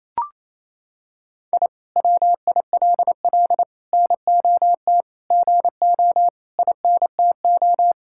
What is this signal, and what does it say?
Morse code, heard as a keyed tone, about 28 wpm, 700 Hz. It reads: I WILL NOT GO INTO
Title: Sentences from Top 1000 Words - Review Part 6 - 28wpm